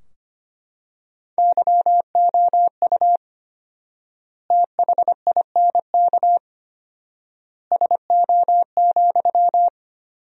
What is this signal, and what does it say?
Morse code, heard as a keyed tone, about 25 wpm, 700 Hz. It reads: YOU THINK SO,
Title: 4x Word-Spacing - Sentences from Top 200 Words - 25wpm